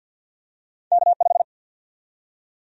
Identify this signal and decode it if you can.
Morse code, heard as a keyed tone, about 51 wpm, 700 Hz. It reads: K5
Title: USA Call Sign Prefixes - 50wpm